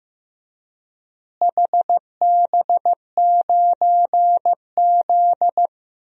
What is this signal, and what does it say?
Morse code, heard as a keyed tone, about 15 wpm, 700 Hz. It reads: HB9Z
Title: DXCC-100 Call Signs - 4 Characters - Repeated 3x - 15wpm